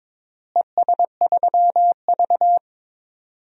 Morse code, heard as a keyed tone, about 22 wpm, 700 Hz. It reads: ES3V
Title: DXCC-100 Call Signs - 4 Characters - Repeated 2x - 22wpm